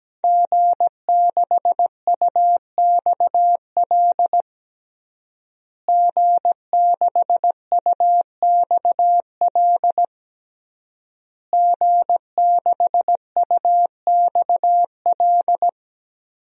Morse code, heard as a keyed tone, about 17 wpm, 700 Hz. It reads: G6UXL G6UXL G6UXL
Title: DXCC-100 Call Signs - 5 Characters - Repeated 3x - 17wpm